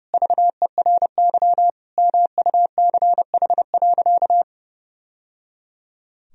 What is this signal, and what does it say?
Morse code, heard as a keyed tone, about 30 wpm, 700 Hz. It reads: VERY MUCH.
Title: Sentences from Top 1000 Words - Review Part 1 - 30wpm